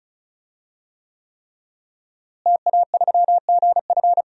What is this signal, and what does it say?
Morse code, heard as a keyed tone, about 36 wpm, 700 Hz. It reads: TA3GF
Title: DXCC-100 Call Signs - 5 Characters - Repeated 3x - 35wpm